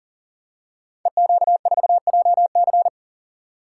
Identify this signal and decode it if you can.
Morse code, heard as a keyed tone, about 42 wpm, 700 Hz. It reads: EQ4JC
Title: Code Groups - 5 Letter:Numbers - 40wpm